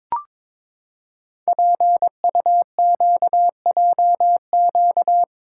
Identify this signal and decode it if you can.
Morse code, heard as a keyed tone, about 22 wpm, 700 Hz. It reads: PUQJQ
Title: Code Groups - 5 Letters - 22wpm